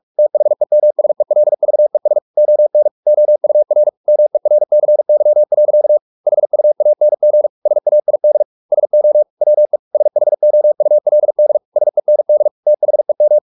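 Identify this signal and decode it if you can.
Morse code, heard as a keyed tone, about 44 wpm, 600 Hz. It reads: THEMSELVES ON OUR MERCY. HUANG SAID SO WE SHOULD SEND THEM
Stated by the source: Morse practice at 45 wpm, the novel Never by Ken Follett in English, from CW med Gnister.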